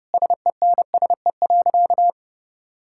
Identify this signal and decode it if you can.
Morse code, heard as a keyed tone, about 30 wpm, 700 Hz. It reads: SENSE.
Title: Sentences from Top 1600 Words 30wpm